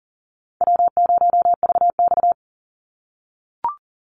Morse code, heard as a keyed tone, about 40 wpm, 700 Hz. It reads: W0VX
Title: Easy Call Signs - 40wpm